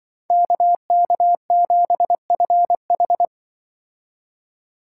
Morse code, heard as a keyed tone, about 25 wpm, 700 Hz. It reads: KK7FH
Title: Speed Racing - USA Call Signs 20wpm